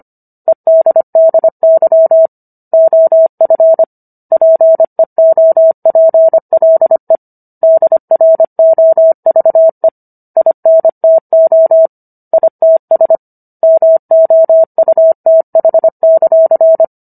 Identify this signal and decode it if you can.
Morse code, heard as a keyed tone, about 25 wpm, 650 Hz. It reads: EDDY OF PEOPLE DROVE INTO ITS MOUTH;
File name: Book One / 16-Chapter 16 - 25 WPM